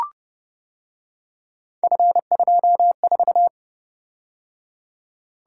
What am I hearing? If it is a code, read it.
Morse code, heard as a keyed tone, about 29 wpm, 700 Hz. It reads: F24